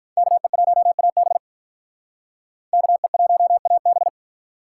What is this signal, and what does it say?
Morse code, heard as a keyed tone, about 44 wpm, 700 Hz. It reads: KE1AB KE1AB